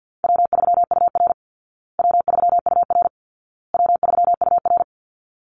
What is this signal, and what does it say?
Morse code, heard as a keyed tone, about 50 wpm, 700 Hz. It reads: W3UR W3UR W3UR
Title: Easy Call Signs Repeated 3x - 50wpm